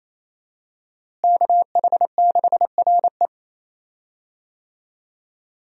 Morse code, heard as a keyed tone, about 28 wpm, 700 Hz. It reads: KH6RE
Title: USA Call Signs - 5 Characters - Repeated 2x - 28wpm